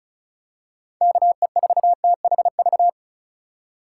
Morse code, heard as a keyed tone, about 36 wpm, 700 Hz. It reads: KE4THV
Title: USA Call Signs - 6 Characters - 3x Character Spacing - 35wpm